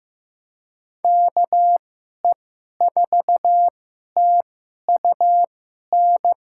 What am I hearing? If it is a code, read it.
Morse code, heard as a keyed tone, about 15 wpm, spaced as for 11 wpm, 700 Hz. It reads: KE4TUN
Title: USA Call Signs - 6 Characters - 2x Character Spacing - 15wpm